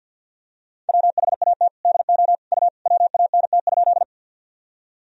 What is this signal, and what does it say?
Morse code, heard as a keyed tone, about 49 wpm, 700 Hz. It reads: WHAT DO U WANT?
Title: Sentences from Top 200 Words and Abbreviations 50wpm